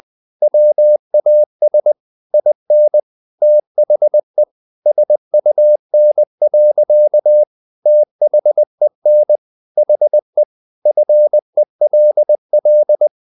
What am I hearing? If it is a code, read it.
Morse code, heard as a keyed tone, about 20 wpm, 600 Hz. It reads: WAS IN THE SUN. THEN HE FELL
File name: the old man and the sea part 2 20wpm